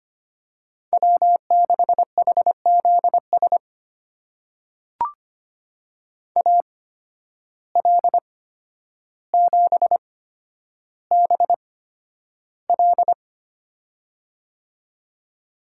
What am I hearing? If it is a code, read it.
Morse code, heard as a keyed tone, about 25 wpm, spaced as for 11 wpm, 700 Hz. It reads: W6HZS AL7BL
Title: USA Call Signs - 5 Characters - 8x Character Spacing - 25wpm